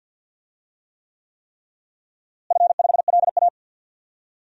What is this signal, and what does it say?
Morse code, heard as a keyed tone, about 52 wpm, 700 Hz. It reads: F5LA
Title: DXCC-100 Call Signs - 4 Characters - Repeated 3x - 50wpm